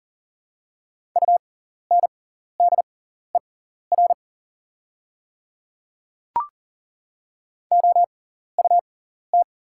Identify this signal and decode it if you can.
Morse code, heard as a keyed tone, about 41 wpm, spaced as for 14 wpm, 700 Hz. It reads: UNDER OUT
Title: 6x Character Spacing - Top 200 Words - 40wpm